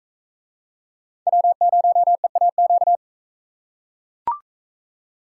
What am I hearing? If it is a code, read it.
Morse code, heard as a keyed tone, about 41 wpm, 700 Hz. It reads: W0EAQ